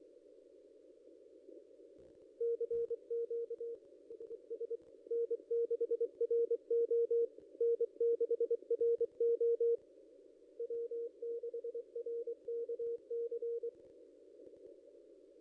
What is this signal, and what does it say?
Morse code, heard as a keyed tone, about 24 wpm, 470 Hz. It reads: CQ SS N6RO N6RO W6RKC
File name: contests / ss-cw-k5tr-op-n5rz -0800